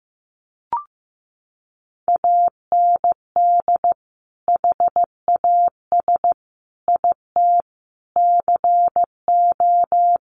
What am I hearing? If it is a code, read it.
Morse code, heard as a keyed tone, about 15 wpm, 700 Hz. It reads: AND HAS IT CO
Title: Sentences from Top 100 Words 15wpm